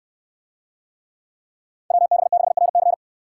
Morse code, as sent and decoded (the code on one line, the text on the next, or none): .-- -... -.... .-. -.-.
WB6RC